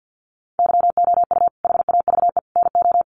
Morse code, heard as a keyed tone, about 48 wpm, 700 Hz. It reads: YOU HAVE NO
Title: Sentences from Top 1500 Words 50wpm